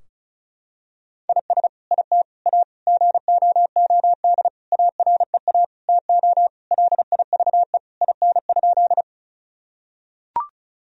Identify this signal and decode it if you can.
Morse code, heard as a keyed tone, about 36 wpm, 700 Hz. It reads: IS IT A GOOD AREA TO LIVE IN?
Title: Sentences from Top 1000 Words - Review Part 2 - 35wpm